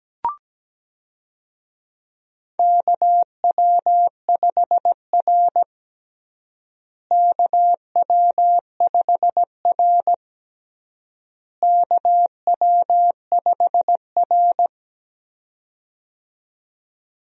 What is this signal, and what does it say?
Morse code, heard as a keyed tone, about 17 wpm, 700 Hz. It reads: KW5R KW5R KW5R
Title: USA Call Signs - 4 Characters - Repeated 3x - 17wpm